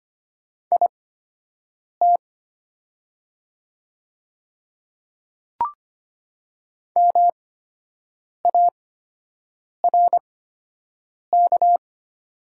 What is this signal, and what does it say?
Morse code, heard as a keyed tone, about 25 wpm, spaced as for 7 wpm, 700 Hz. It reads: IT MARK
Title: 8x Character Spacing - Top 300 Words - 25wpm